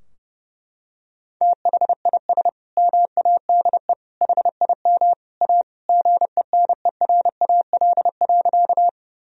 Morse code, heard as a keyed tone, about 30 wpm, 700 Hz. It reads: THIS MADE HIM A GENERAL.
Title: Sentences from Top 1000 Words - Review Part 3 - 30wpm